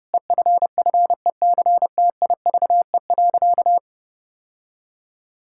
Morse code, heard as a keyed tone, about 28 wpm, 700 Hz. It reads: EFFECTIVE.
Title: Sentences from Top 1000 Words - Review Part 6 - 30wpm